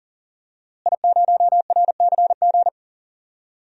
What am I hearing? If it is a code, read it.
Morse code, heard as a keyed tone, about 40 wpm, 700 Hz. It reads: I0RCG
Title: DXCC-100 Call Signs - 5 Characters - Repeated 2x - 40wpm